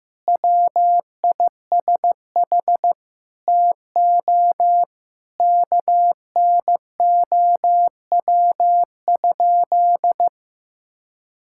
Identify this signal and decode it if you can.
Morse code, heard as a keyed tone, about 16 wpm, 700 Hz. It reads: WISH TO KNOW?
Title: Sentences from Top 1600 Words 15wpm